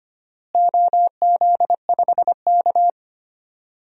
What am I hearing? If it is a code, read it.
Morse code, heard as a keyed tone, about 25 wpm, 700 Hz. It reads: OZ5K
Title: DXCC-100 Call Signs - 4 Characters - Repeated 3x - 25wpm